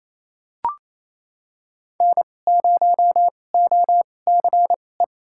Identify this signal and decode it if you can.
Morse code, heard as a keyed tone, about 28 wpm, spaced as for 20 wpm, 700 Hz. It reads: N0OCE
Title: USA Call Signs - 5 Characters - 2x Character Spacing - 28wpm